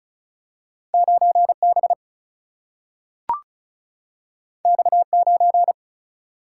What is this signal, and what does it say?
Morse code, heard as a keyed tone, about 36 wpm, 700 Hz. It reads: 9B X9
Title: Single-Letter Single-Number 35wpm